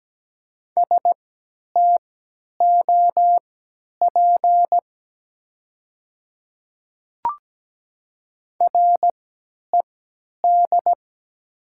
Morse code, heard as a keyed tone, about 17 wpm, spaced as for 10 wpm, 700 Hz. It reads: STOP RED
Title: 3x Character Spacing - Top 500 Words - 17wpm